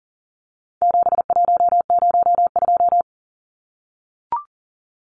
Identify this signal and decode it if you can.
Morse code, heard as a keyed tone, about 40 wpm, 700 Hz. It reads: 7102
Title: Numbers - 4-Digit 40wpm